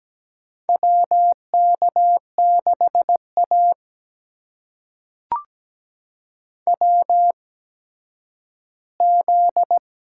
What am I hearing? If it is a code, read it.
Morse code, heard as a keyed tone, about 17 wpm, spaced as for 10 wpm, 700 Hz. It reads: WK6A WZ
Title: USA Call Signs - 4 Characters - 8x Character Spacing - 17wpm